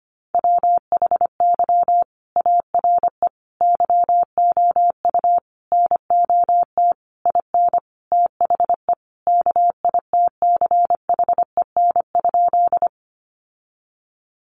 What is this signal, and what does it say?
Morse code, heard as a keyed tone, about 25 wpm, 700 Hz. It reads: WHY ARE YOU NOT IN THE KITCHEN?
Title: Sentences from Top 1400 Words 25wpm